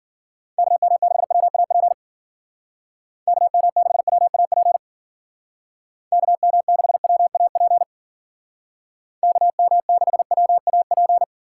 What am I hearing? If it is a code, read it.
Morse code, heard as a keyed tone, about 53 wpm, 700 Hz. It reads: KM6WAP KM6WAP KM6WAP KM6WAP